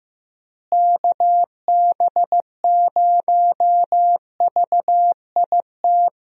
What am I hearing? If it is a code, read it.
Morse code, heard as a keyed tone, about 15 wpm, 700 Hz. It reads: KB0VIT
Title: USA Call Signs - 6 Characters - Repeated 3x - 15wpm